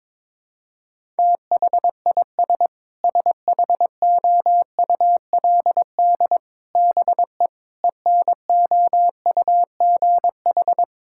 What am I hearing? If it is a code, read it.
Morse code, heard as a keyed tone, about 22 wpm, 700 Hz. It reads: THIS SHOULD BE ENOUGH
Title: Sentences from Top 1000 Words - Review Part 6 - 22wpm